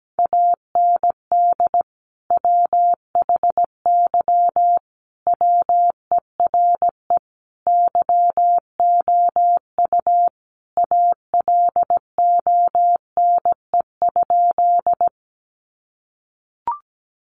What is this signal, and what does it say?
Morse code, heard as a keyed tone, about 17 wpm, 700 Hz. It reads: AND WHY WERE YOU ALONE?
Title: Sentences from Top 1000 Words 17wpm